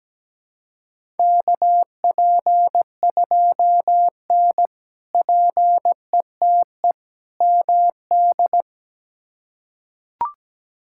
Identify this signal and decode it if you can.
Morse code, heard as a keyed tone, about 17 wpm, 700 Hz. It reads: KP2N PETE MD
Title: K1USN SST QSO Practice - 17wpm